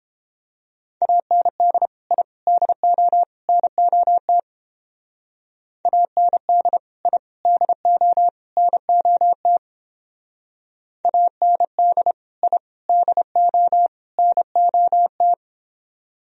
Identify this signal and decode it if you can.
Morse code, heard as a keyed tone, about 28 wpm, 700 Hz. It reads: AND I DO NOT AND I DO NOT AND I DO NOT